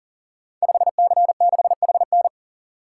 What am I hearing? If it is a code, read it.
Morse code, heard as a keyed tone, about 44 wpm, 690 Hz. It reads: 5C6HN
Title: Code Groups - 5 Letter:Numbers - 40wpm